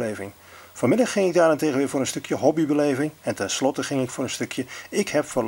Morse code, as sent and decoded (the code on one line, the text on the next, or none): none